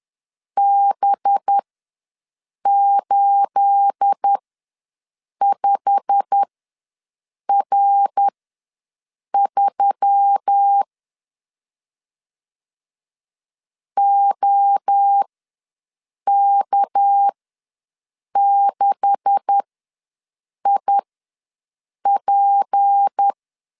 Morse code, heard as a keyed tone, about 10 wpm, spaced as for 6 wpm, 800 Hz. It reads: B85R3 OK6IP
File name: LEZIONI / 9 286 ws 10wpm 4032